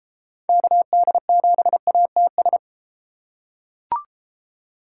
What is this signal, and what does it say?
Morse code, heard as a keyed tone, about 32 wpm, 700 Hz. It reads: KD7ATS